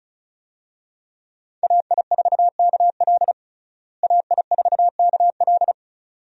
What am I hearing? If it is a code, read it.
Morse code, heard as a keyed tone, about 34 wpm, 700 Hz. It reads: AI4KL AI4KL
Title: USA Call Signs - 5 Characters - Repeated 2x - 35wpm